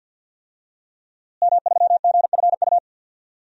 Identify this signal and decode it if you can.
Morse code, heard as a keyed tone, about 52 wpm, 700 Hz. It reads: M3GFU
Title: DXCC-100 Call Signs - 5 Characters - Repeated 2x - 50wpm